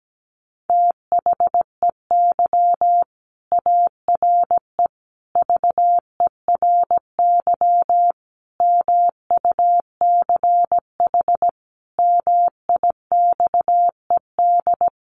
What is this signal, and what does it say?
Morse code, heard as a keyed tone, about 17 wpm, 700 Hz. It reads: THEY ARE VERY MUCH MIXED
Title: Sentences from Top 1500 Words 17wpm